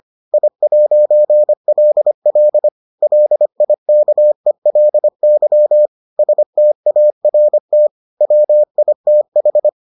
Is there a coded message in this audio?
Yes: Morse code from 0.3 to 9.7 s, heard as a keyed tone, about 25 wpm, 600 Hz. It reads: I'LL LIKELY START WITH